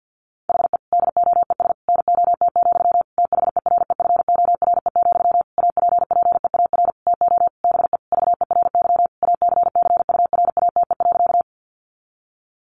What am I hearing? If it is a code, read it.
Morse code, heard as a keyed tone, about 50 wpm, 700 Hz. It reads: HE DOES NOT, THEREFORE, APPEAR TO BE VERY ACCURATE.
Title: Sentences from Top 1200 Words 50wpm